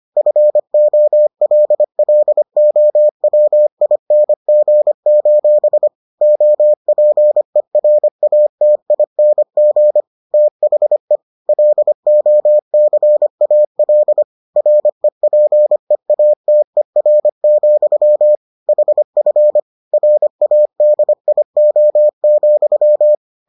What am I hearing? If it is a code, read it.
Morse code, heard as a keyed tone, about 25 wpm, 600 Hz. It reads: FOLLOWING: OPERATING THE LOCAL REPEATER, HF RADIO,